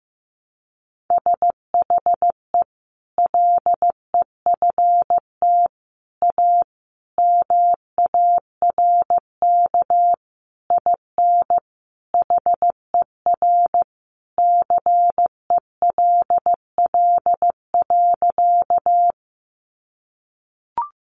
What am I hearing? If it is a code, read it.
Morse code, heard as a keyed tone, about 15 wpm, 700 Hz. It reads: SHE LEFT A MARK IN HER CELL.
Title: Sentences from Top 1100 Words 15wpm